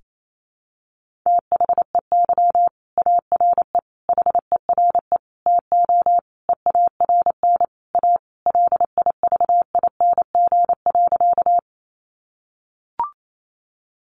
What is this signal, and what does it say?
Morse code, heard as a keyed tone, about 28 wpm, 700 Hz. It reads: THEY ARE HERE TO EARN A LIVING.
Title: Sentences from Top 1500 Words 28wpm